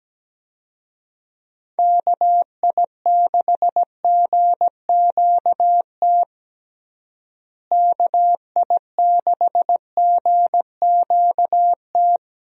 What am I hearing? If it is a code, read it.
Morse code, heard as a keyed tone, about 17 wpm, 700 Hz. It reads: KI6GQT KI6GQT
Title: USA Call Signs - 6 Characters - Repeated 2x - 17wpm